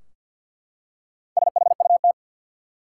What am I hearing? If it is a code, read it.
Morse code, heard as a keyed tone, about 52 wpm, 700 Hz. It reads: SHUT
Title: Four Letter Words - 50wpm